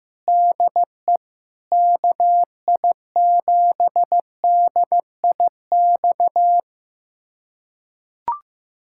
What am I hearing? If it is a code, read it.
Morse code, heard as a keyed tone, about 15 wpm, 700 Hz. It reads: DE KI7DIX
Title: USA Callsign DE Callsign - 6 Characters - 15wpm